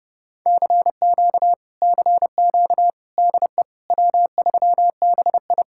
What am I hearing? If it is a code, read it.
Morse code, heard as a keyed tone, about 32 wpm, 700 Hz. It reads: CQ CQ DE W3BI